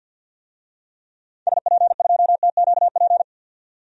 Morse code, heard as a keyed tone, about 51 wpm, 700 Hz. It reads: SP2TQP
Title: DXCC-100 Call Signs - 6 Characters - 50wpm